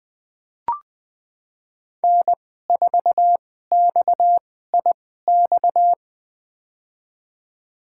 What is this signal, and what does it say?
Morse code, heard as a keyed tone, about 20 wpm, spaced as for 15 wpm, 700 Hz. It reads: N4XIX
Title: USA Call Signs - 5 Characters - 2x Character Spacing - 20wpm